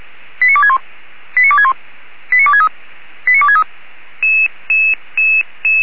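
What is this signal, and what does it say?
Five-tone selective call (ZVEI), heard as quick runs of five tones, tones of about 80 ms. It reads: 89261 89261 89263 89263
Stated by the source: ZVEI five-tone selective call received on SDR.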